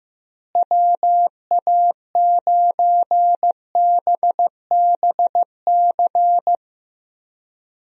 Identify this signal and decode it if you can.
Morse code, heard as a keyed tone, about 15 wpm, 700 Hz. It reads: WA9BBC
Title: USA Call Signs - 6 Characters - Repeated 2x - 15wpm